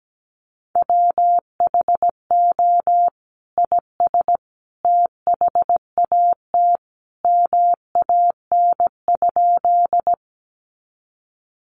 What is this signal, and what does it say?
Morse code, heard as a keyed tone, about 17 wpm, 700 Hz. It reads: WHO IS THAT MAN?
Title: Sentences from Top 200 Words 17wpm